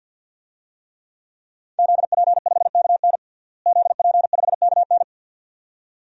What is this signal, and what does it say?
Morse code, heard as a keyed tone, about 51 wpm, 700 Hz. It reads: ZP5KN ZP5KN
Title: DXCC-100 Call Signs - 5 Characters - Repeated 2x - 50wpm